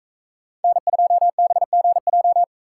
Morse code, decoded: N2BGJ

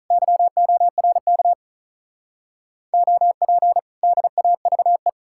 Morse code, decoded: YORK OP DAVE

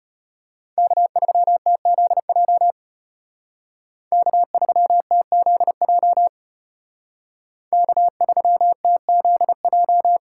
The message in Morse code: K3TZJ K3TZJ K3TZJ

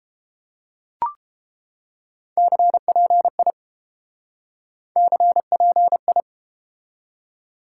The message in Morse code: CPI CPI

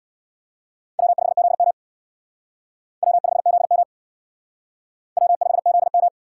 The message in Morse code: W5ZM W5ZM W5ZM

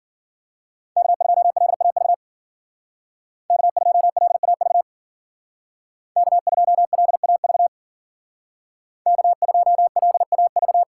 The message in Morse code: K2LAV K2LAV K2LAV K2LAV

